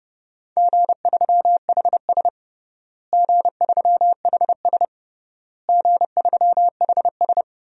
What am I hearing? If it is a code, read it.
Morse code, heard as a keyed tone, about 30 wpm, 700 Hz. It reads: G3HS G3HS G3HS
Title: DXCC-100 Call Signs - 4 Characters - Repeated 3x - 30wpm